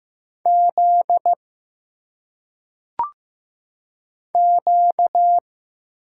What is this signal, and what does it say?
Morse code, heard as a keyed tone, about 15 wpm, 700 Hz. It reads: Z Q
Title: Single Letters - Phonetic Pronunciation 15wpm